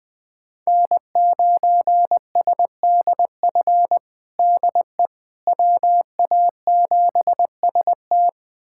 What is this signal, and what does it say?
Morse code, heard as a keyed tone, about 20 wpm, 700 Hz. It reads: N9SDF DE WA7ST
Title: USA Callsign DE Callsign - 5 Characters - 20wpm